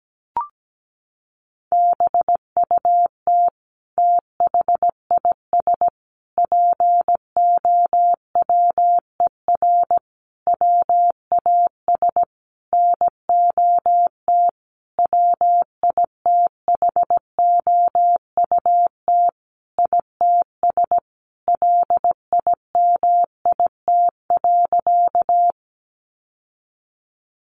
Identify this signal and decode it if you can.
Morse code, heard as a keyed tone, about 17 wpm, 700 Hz. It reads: BUT THIS POWER WAS NOT WITHOUT ITS LIMIT.
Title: Sentences from Top 1500 Words 17wpm